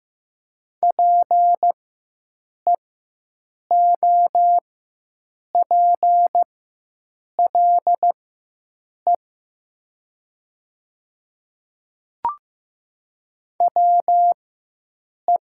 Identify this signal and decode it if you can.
Morse code, heard as a keyed tone, about 15 wpm, spaced as for 7 wpm, 700 Hz. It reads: PEOPLE WE
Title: 4x Character Spacing - Top 100 Words - 15wpm